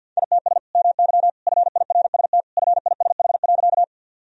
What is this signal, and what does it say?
Morse code, heard as a keyed tone, about 52 wpm, 700 Hz. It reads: ITS MY FIRST FISH.
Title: Sentences from Top 1000 Words - Review Part 1 - 50wpm